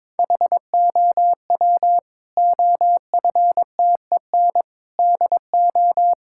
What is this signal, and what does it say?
Morse code, heard as a keyed tone, about 22 wpm, 700 Hz. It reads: HOW OFTEN DO
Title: Sentences from Top 1000 Words - Review Part 5 - 22wpm